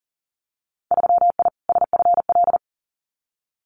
Morse code, heard as a keyed tone, about 38 wpm, 700 Hz. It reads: 3I SFL
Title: Winter Field Day Exchanges 40wpm